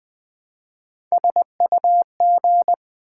Morse code, heard as a keyed tone, about 20 wpm, 700 Hz. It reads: SUG
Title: Build A Word - 1 Letter At A Time - 3 to 5-Letter Words - 20wpm